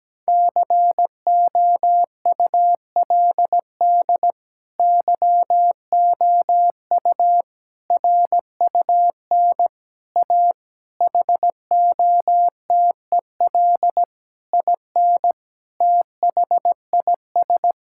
Morse code, heard as a keyed tone, about 17 wpm, 700 Hz. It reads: COULD YOU RUN A HOTEL IN THIS
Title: Sentences from Top 1600 Words 17wpm